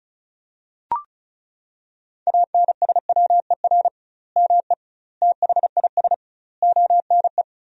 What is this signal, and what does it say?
Morse code, heard as a keyed tone, about 36 wpm, 700 Hz. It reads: ANSWER ME THIS ONE
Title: 2x Word-Spacing - Sentences from Top 400 Words - 35wpm